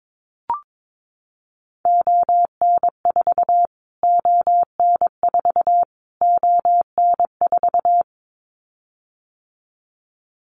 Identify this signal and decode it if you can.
Morse code, heard as a keyed tone, about 22 wpm, 700 Hz. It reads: ON4 ON4 ON4
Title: Callsign Prefixes for Top 25 DXCC - Repeated 3x - 22wpm